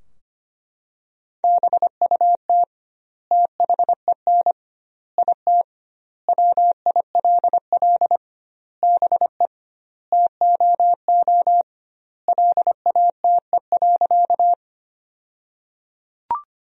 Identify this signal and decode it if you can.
Morse code, heard as a keyed tone, about 25 wpm, 700 Hz. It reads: BUT THEN IT WILL BE TOO LATE.
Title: 2x Word-Spacing - Sentences from Top 300 Words - 25wpm